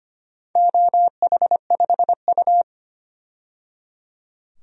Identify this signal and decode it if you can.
Morse code, heard as a keyed tone, about 25 wpm, 700 Hz. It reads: OH5U
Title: DXCC-100 Call Signs - 4 Characters - 25wpm